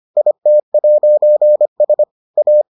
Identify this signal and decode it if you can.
Morse code, heard as a keyed tone, about 25 wpm, 600 Hz. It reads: IT'S A